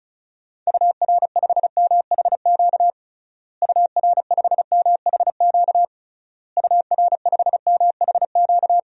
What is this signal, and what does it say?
Morse code, heard as a keyed tone, about 36 wpm, 700 Hz. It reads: UR5MHQ UR5MHQ UR5MHQ
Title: DXCC-100 Call Signs - 6 Characters - Repeated 3x - 35wpm